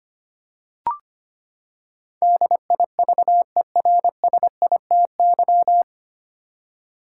Morse code, heard as a keyed tone, about 25 wpm, 700 Hz. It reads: DIVERSITY
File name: Nine Letter Words - 25wpm